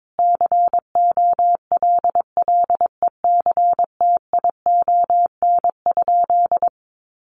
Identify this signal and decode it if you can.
Morse code, heard as a keyed tone, about 22 wpm, 700 Hz. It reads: COLLECTION?
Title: Sentences from Top 1200 Words 22wpm